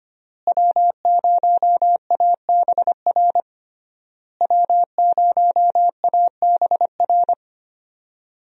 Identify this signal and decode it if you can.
Morse code, heard as a keyed tone, about 25 wpm, 700 Hz. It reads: W0ABR W0ABR
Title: USA Call Signs - 5 Characters - Repeated 3x - 25wpm